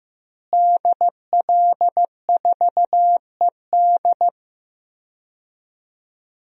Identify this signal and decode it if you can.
Morse code, heard as a keyed tone, about 15 wpm, 700 Hz. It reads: DL4ED
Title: DXCC-100 Call Signs - 5 Characters - Repeated 2x - 15wpm